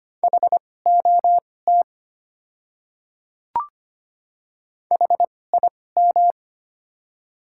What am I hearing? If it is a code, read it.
Morse code, heard as a keyed tone, about 25 wpm, spaced as for 18 wpm, 700 Hz. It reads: HOT HIM
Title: 2x Character Spacing - Top 500 Words - 25wpm